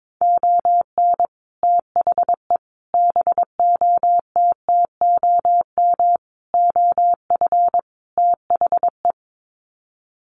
Morse code, heard as a keyed tone, about 22 wpm, 700 Hz. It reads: ON THE BOTTOM OF THE
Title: Sets of 5 Words Encore2 22wpm